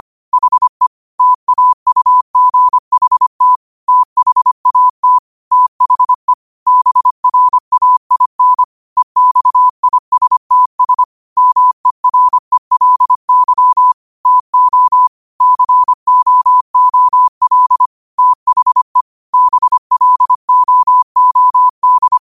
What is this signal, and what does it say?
Morse code, heard as a keyed tone, about 25 wpm, 1000 Hz. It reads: HE TAUGHT THAT THE BRAIN EXISTS MERELY TO COOL THE BLOOD
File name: qotdfiles / 25wpm